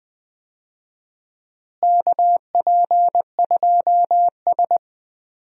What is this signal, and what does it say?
Morse code, heard as a keyed tone, about 20 wpm, 700 Hz. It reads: KP2S
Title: DXCC-100 Call Signs - 4 Characters - Repeated 3x - 20wpm